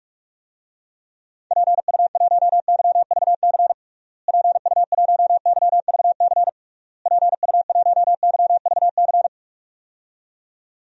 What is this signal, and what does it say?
Morse code, heard as a keyed tone, about 41 wpm, 700 Hz. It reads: PU1YVC PU1YVC PU1YVC
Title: DXCC-100 Call Signs - 6 Characters - Repeated 3x - 45wpm